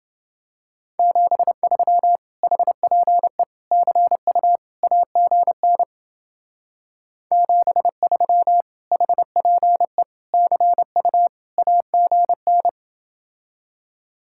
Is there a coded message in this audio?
Yes: Morse code from 1.0 to 12.7 s, heard as a keyed tone, about 29 wpm, 700 Hz. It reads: 73 HPE CU AGN 73 HPE CU AGN